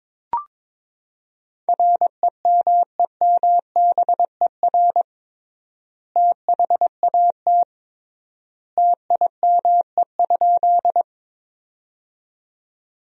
Morse code, heard as a keyed tone, about 21 wpm, 700 Hz. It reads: REMEMBER THAT TIME?
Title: 3x Word-Spacing - Sentences from Top 400 Words - 22wpm